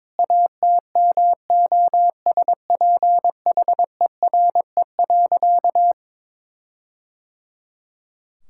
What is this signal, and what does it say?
Morse code, heard as a keyed tone, about 22 wpm, 700 Hz. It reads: ATMOSPHERE.
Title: Sentences from Top 1600 Words 22wpm